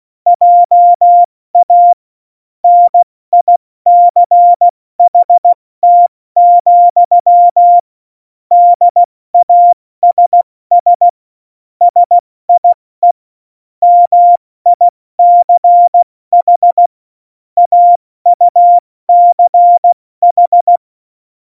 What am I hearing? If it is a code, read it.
Morse code, heard as a keyed tone, about 16 wpm, 700 Hz. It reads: JA NICHT, DASS SIE MICH AUCH
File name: Morsekurs / Witze / Witz 09. Juli 2019, 70 F80